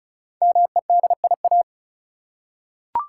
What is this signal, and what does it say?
Morse code, heard as a keyed tone, about 34 wpm, 700 Hz. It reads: MEDIA
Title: Five Letter Words - 35wpm